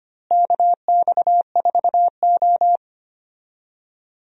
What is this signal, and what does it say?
Morse code, heard as a keyed tone, about 25 wpm, 700 Hz. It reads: KX4O
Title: USA Call Signs - 4 Characters - Repeated 2x - 25wpm